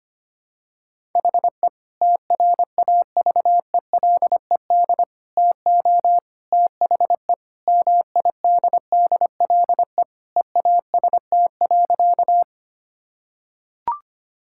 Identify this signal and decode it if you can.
Morse code, heard as a keyed tone, about 25 wpm, 700 Hz. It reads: HE TRAVELED TO THE MIDDLE EAST.
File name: practice / 25wpm / Sentences with 3-Word Phrases - 25wpm